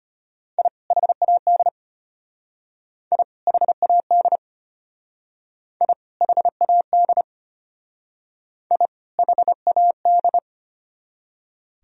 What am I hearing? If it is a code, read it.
Morse code, heard as a keyed tone, about 33 wpm, 700 Hz. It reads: I HAD I HAD I HAD I HAD